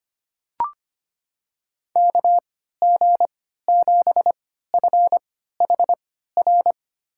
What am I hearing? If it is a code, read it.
Morse code, heard as a keyed tone, about 25 wpm, spaced as for 14 wpm, 700 Hz. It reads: KG7FHR
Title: USA Call Signs - 3x Character Spacing - 25wpm